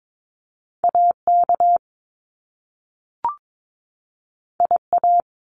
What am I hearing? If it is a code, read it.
Morse code, heard as a keyed tone, about 22 wpm, 700 Hz. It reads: AK IA
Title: US States 22wpm